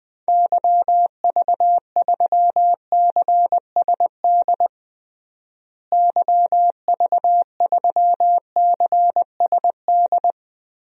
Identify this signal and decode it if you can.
Morse code, heard as a keyed tone, about 20 wpm, 700 Hz. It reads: YV3CSD YV3CSD
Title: DXCC-100 Call Signs - 6 Characters - Repeated 2x - 20wpm